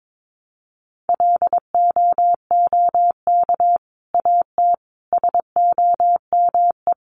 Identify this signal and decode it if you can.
Morse code, heard as a keyed tone, about 22 wpm, 700 Hz. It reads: LOOK AT SOME
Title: Sets of 5 Words Encore 22wpm